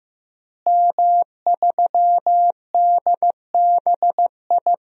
Morse code, heard as a keyed tone, about 16 wpm, 700 Hz. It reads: M3DBI
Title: DXCC-100 Call Signs - 5 Characters - Repeated 2x - 15wpm